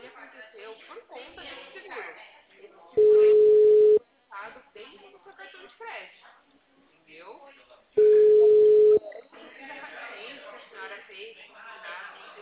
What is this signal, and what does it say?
Ringback tone (Europe, 425 Hz), with speech.